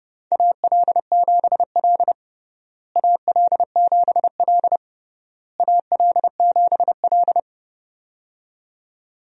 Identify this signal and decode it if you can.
Morse code, heard as a keyed tone, about 32 wpm, 700 Hz. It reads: AL7L AL7L AL7L
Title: USA Call Signs - 4 Characters - Repeated 3x - 30wpm